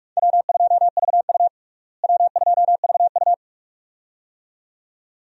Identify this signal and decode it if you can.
Morse code, heard as a keyed tone, about 46 wpm, 700 Hz. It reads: W2VU W2VU
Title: USA Call Signs - 4 Characters - Repeated 2x - 45wpm